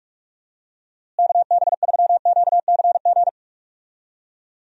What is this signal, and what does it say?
Morse code, heard as a keyed tone, about 42 wpm, 700 Hz. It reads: KB3QCG
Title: USA Call Signs - 6 Characters - Repeated 3x - 45wpm